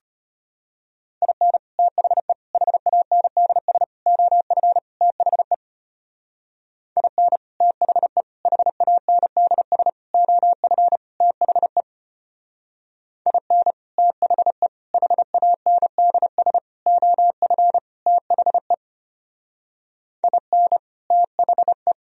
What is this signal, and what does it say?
Morse code, heard as a keyed tone, about 30 wpm, 700 Hz. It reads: IN THE HANDS OF THE IN THE HANDS OF THE IN THE HANDS OF THE IN THE
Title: Speed-Racing - Sets of 5 Words 25wpm